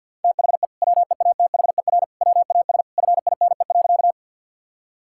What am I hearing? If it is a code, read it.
Morse code, heard as a keyed tone, about 50 wpm, 700 Hz. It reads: THE WEATHER WAS FINE.
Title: Sentences from Top 1000 Words - Review Part 5 - 50wpm